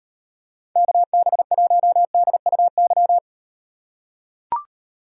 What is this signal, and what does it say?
Morse code, heard as a keyed tone, about 38 wpm, 700 Hz. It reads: KB1DUY